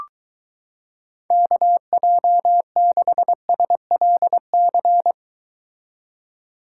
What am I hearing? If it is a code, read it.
Morse code, heard as a keyed tone, about 23 wpm, 700 Hz. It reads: KJ6SLC